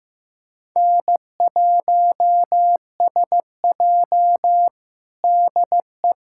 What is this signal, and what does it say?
Morse code, heard as a keyed tone, about 15 wpm, 700 Hz. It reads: N1SJ DE
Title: USA Callsign DE Callsign - 4 Characters - 15wpm